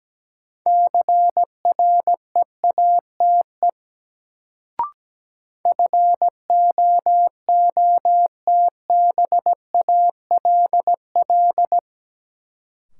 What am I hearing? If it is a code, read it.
Morse code, heard as a keyed tone, about 17 wpm, 700 Hz. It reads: CREATE FOOTBALL
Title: Sentences from Top 1600 Words - New Word Drill Alt - 17wpm